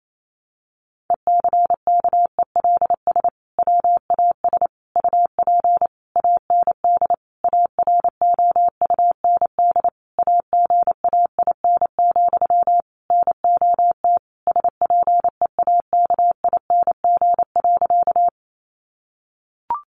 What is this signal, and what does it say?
Morse code, heard as a keyed tone, about 28 wpm, 700 Hz. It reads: ECKELS WAS UP AND AROUND AGAIN, NOT SPEAKING.
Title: A Sound Of Thunder 28wpm